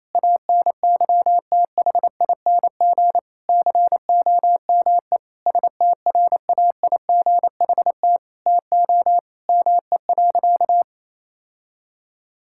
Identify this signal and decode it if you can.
Morse code, heard as a keyed tone, about 28 wpm, 700 Hz. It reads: ANYTHING COME STRAIGHT TO ME.
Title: Sentences from Top 1000 Words - Review Part 6 - 28wpm